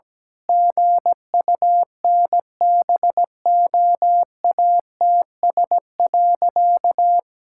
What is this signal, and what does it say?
Morse code, heard as a keyed tone, about 17 wpm, 700 Hz. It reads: GUNBOATS.
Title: Pangram Challenge v2 - 17wpm